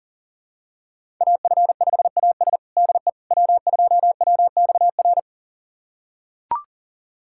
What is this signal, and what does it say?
Morse code, heard as a keyed tone, about 40 wpm, 700 Hz. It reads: AF5AS DE W2WXR
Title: USA Callsign DE Callsign - 5 Characters - 40wpm